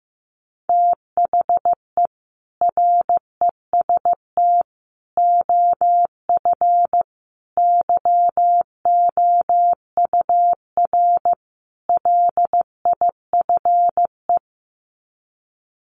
Morse code, heard as a keyed tone, about 15 wpm, 700 Hz. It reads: THE REST OF YOUR LIFE
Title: Sets of 5 Words 15wpm